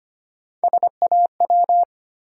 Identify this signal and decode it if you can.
Morse code, heard as a keyed tone, about 25 wpm, 700 Hz. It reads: SAW
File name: practice / 25wpm / Easy Words with Hints - 25wpm